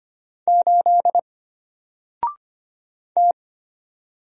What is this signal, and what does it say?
Morse code, heard as a keyed tone, about 25 wpm, 700 Hz. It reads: 8 T
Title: Single Letter-Number Alt 25wpm